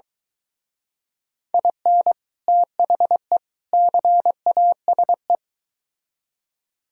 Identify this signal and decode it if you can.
Morse code, heard as a keyed tone, about 22 wpm, 700 Hz. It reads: IN THE CASE